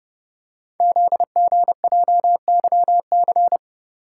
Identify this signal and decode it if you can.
Morse code, heard as a keyed tone, about 30 wpm, 690 Hz. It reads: ZGJYC
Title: Code Groups - 5 Letter:Numbers - 30wpm